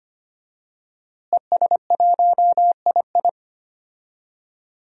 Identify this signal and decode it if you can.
Morse code, heard as a keyed tone, about 25 wpm, 700 Hz. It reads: ES1II
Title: DXCC-100 Call Signs - 5 Characters - Repeated 3x - 25wpm